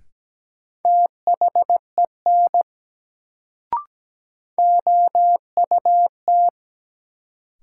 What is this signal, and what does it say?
Morse code, heard as a keyed tone, about 17 wpm, 700 Hz. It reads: THEN OUT
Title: Sentences from Top 100 Words - New Word Drill - 17wpm